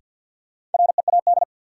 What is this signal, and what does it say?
Morse code, heard as a keyed tone, about 53 wpm, 700 Hz. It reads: READ
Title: Sentences from Top 200 Words - New Word Drill - 50wpm